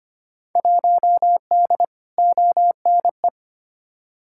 Morse code, heard as a keyed tone, about 25 wpm, 700 Hz. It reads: 1D ONE
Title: Field Day Exchanges - 25wpm